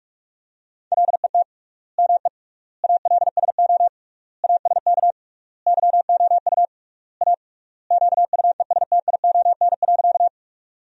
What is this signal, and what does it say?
Morse code, heard as a keyed tone, about 46 wpm, 700 Hz. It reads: LET ME ALSO ASK YOU A QUESTION.
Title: 3x Word-Spacing - Sentences from Top 400 Words - 45wpm